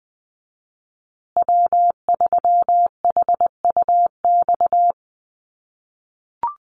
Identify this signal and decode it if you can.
Morse code, heard as a keyed tone, about 20 wpm, 700 Hz. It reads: W3HUX
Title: USA Call Signs Repeated 2x 20wpm